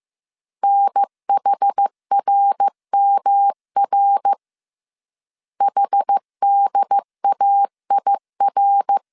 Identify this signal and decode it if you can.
Morse code, heard as a keyed tone, about 15 wpm, 800 Hz. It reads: NHRMR HDAIR
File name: CW / DET / 1 3 1500C S A 14wpm 2447